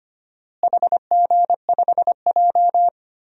Morse code, heard as a keyed tone, about 25 wpm, 700 Hz. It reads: HG5J
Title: DXCC-100 Call Signs - 4 Characters - Repeated 3x - 25wpm